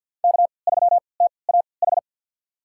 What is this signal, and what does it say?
Morse code, heard as a keyed tone, about 47 wpm, spaced as for 28 wpm, 690 Hz. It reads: K3TAH
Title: USA Call Signs - 3x Character Spacing - 50wpm